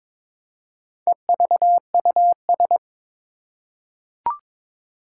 Morse code, heard as a keyed tone, about 22 wpm, 700 Hz. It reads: EVUS
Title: Four Challenging Letter-Number Combinations 22wpm